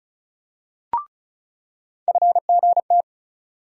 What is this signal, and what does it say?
Morse code, heard as a keyed tone, about 36 wpm, 700 Hz. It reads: FGT
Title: USA Call Sign Postfixes - 3 Letters 35wpm